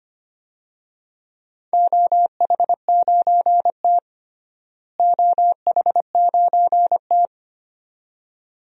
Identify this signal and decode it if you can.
Morse code, heard as a keyed tone, about 25 wpm, 700 Hz. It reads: OH9T OH9T
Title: DXCC-100 Call Signs - 4 Characters - Repeated 2x - 25wpm